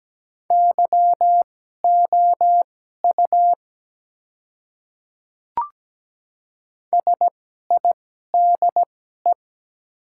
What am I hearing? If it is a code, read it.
Morse code, heard as a keyed tone, about 17 wpm, spaced as for 12 wpm, 700 Hz. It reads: YOU SIDE
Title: 2x Character Spacing - Top 100 Words - 17wpm